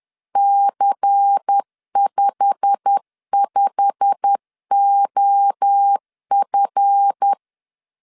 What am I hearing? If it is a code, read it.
Morse code, heard as a keyed tone, about 10 wpm, 800 Hz. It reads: C55OF
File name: LEZIONI / 8 YJ45 / 8 YJ45 10wpm 2549